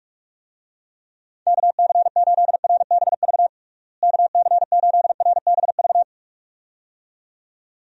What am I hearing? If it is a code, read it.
Morse code, heard as a keyed tone, about 46 wpm, 700 Hz. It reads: KC8RBV KC8RBV
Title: USA Call Signs - 6 Characters - Repeated 2x - 45wpm